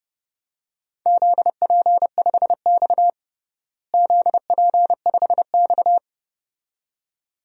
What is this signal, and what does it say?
Morse code, heard as a keyed tone, about 32 wpm, 700 Hz. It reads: ZP5X ZP5X
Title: DXCC-100 Call Signs - 4 Characters - Repeated 2x - 30wpm